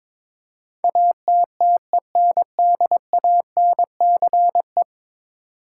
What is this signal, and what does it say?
Morse code, heard as a keyed tone, about 22 wpm, 700 Hz. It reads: ATTENDANCE
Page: morse-code-ninja-practice-22wpm